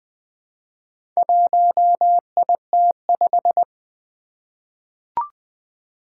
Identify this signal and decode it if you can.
Morse code, heard as a keyed tone, about 20 wpm, 700 Hz. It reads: 1IT5
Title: Four Challenging Letter-Number Combinations 20wpm